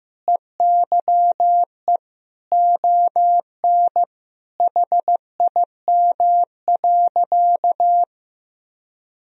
Morse code, heard as a keyed tone, about 15 wpm, 700 Hz. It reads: EYE ON HIM.
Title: Sentences from Top 1000 Words - Review Part 1 - 15wpm